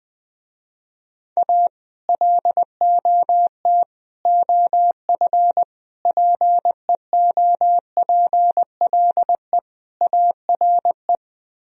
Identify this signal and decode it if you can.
Morse code, heard as a keyed tone, about 20 wpm, 700 Hz. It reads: A LOT OF PEOPLE ARE
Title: Speed-Racing - Sets of 5 Words 20wpm